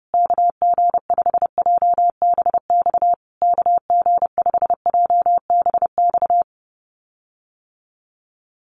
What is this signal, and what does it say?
Morse code, heard as a keyed tone, about 30 wpm, 700 Hz. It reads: KG5JBX KG5JBX
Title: USA Call Signs Repeated 3x 30wpm